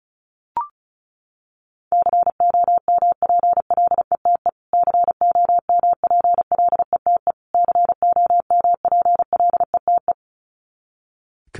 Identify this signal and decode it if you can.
Morse code, heard as a keyed tone, about 34 wpm, 700 Hz. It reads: COMPLETE COMPLETE COMPLETE
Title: Top 500 Words Repeated 35wpm